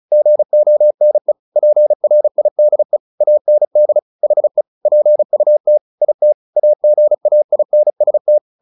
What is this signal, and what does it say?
Morse code, heard as a keyed tone, about 34 wpm, 600 Hz. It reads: GONE PRIDE AND HE PUT IT AGAINST
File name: the old man and the sea 2 35wpm